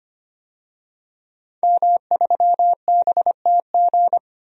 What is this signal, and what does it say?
Morse code, heard as a keyed tone, about 25 wpm, 700 Hz. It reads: M3BTG
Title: DXCC-100 Call Signs - 5 Characters - 25wpm